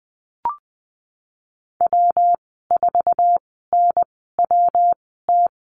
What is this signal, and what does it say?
Morse code, heard as a keyed tone, about 20 wpm, spaced as for 15 wpm, 700 Hz. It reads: W4NWT
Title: 167 Lesson 27g - Number 7 - Call Signs 20wpm